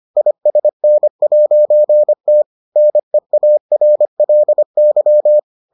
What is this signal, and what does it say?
Morse code, heard as a keyed tone, about 25 wpm, 600 Hz. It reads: ISN'T NEARLY